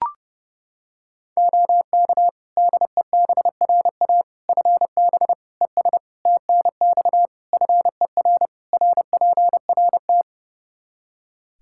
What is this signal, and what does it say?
Morse code, heard as a keyed tone, about 30 wpm, 700 Hz. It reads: OK DEBRA FB ES TNX FER RPRT
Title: Practice QSO - Exchange 2 - Info - 30wpm